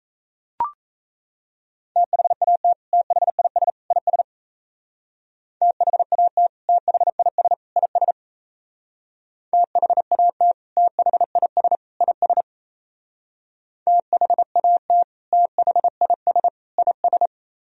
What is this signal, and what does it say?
Morse code, heard as a keyed tone, about 35 wpm, 700 Hz. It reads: THAT THIS IS THAT THIS IS THAT THIS IS THAT THIS IS
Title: Speed-Racing - Sets of 3 Words 28wpm